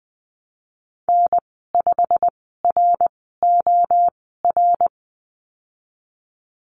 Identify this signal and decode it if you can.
Morse code, heard as a keyed tone, about 20 wpm, spaced as for 15 wpm, 700 Hz. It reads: N5ROR
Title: 167 Lesson 27g - Number 7 - Call Signs 20wpm